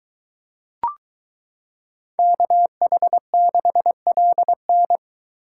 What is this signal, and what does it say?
Morse code, heard as a keyed tone, about 22 wpm, 700 Hz. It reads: KH6LN